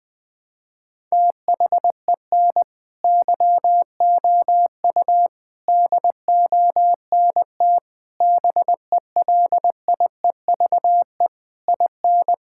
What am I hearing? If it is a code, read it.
Morse code, heard as a keyed tone, about 20 wpm, 700 Hz. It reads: THEN YOU DONT BELIEVE IN
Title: Sentences from Top 1000 Words - Review Part 6 - 20wpm